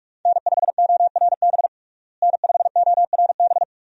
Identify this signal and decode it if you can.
Morse code, heard as a keyed tone, about 41 wpm, 700 Hz. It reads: N5ORB N5ORB
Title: USA Call Signs - 5 Characters - Repeated 2x - 45wpm